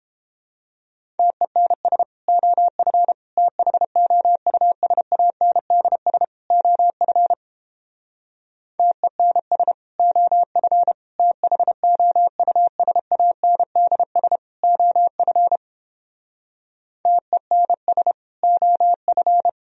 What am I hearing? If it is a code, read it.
Morse code, heard as a keyed tone, about 31 wpm, 700 Hz. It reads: TENS OF THOUSANDS OF TENS OF THOUSANDS OF TENS OF